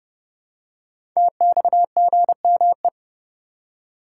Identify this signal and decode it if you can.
Morse code, heard as a keyed tone, about 30 wpm, 700 Hz. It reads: TXGME
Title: Code Groups - 5 Letter:Numbers - 30wpm